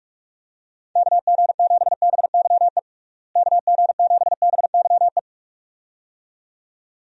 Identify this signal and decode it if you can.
Morse code, heard as a keyed tone, about 46 wpm, 700 Hz. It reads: KG7BYE KG7BYE
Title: USA Call Signs - 6 Characters - Repeated 2x - 45wpm